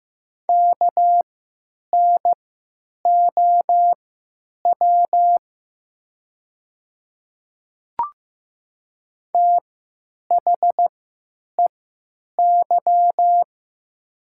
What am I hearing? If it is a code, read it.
Morse code, heard as a keyed tone, about 15 wpm, spaced as for 9 wpm, 700 Hz. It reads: KNOW THEY
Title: 3x Character Spacing - Top 100 Words - 15wpm